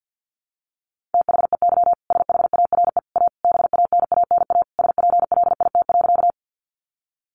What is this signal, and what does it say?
Morse code, heard as a keyed tone, about 50 wpm, 700 Hz. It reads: THEY SHARE A BANANA SPLIT.
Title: Easy Phrases 50wpm